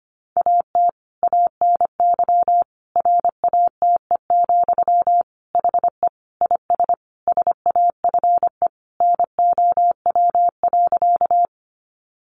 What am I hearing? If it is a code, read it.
Morse code, heard as a keyed tone, about 25 wpm, 700 Hz. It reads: AT ANY RATE, HE IS SAFE NOW.
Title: Sentences from Top 900 Words 25wpm